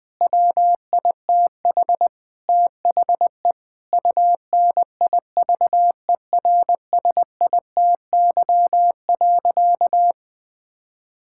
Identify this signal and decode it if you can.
Morse code, heard as a keyed tone, about 20 wpm, 700 Hz. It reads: WITH THE UNIVERSITY.
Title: Sentences from Top 1000 Words - Review Part 6 - 20wpm